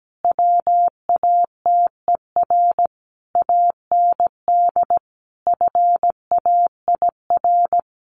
Morse code, heard as a keyed tone, about 17 wpm, 700 Hz. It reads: WATER AND FAIR
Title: Sentences from Top 1400 Words 17wpm